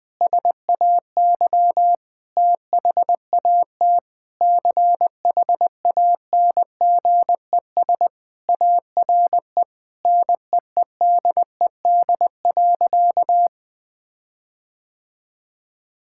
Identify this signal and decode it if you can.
Morse code, heard as a keyed tone, about 20 wpm, 700 Hz. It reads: SAY THAT CHANGES ARE NEEDED.
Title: Sentences with 6-Word Phrases - Spoken Beginning - 20wpm